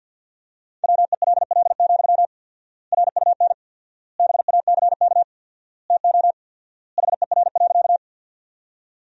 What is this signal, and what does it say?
Morse code, heard as a keyed tone, about 50 wpm, 700 Hz. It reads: WELL, RUN BACK TO HER.
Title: 4x Word-Spacing - Sentences from Top 300 Words - 50wpm